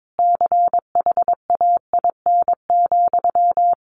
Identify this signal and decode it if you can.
Morse code, heard as a keyed tone, about 22 wpm, 700 Hz. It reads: CHAIN,
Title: Sentences from Top 1500 Words 22wpm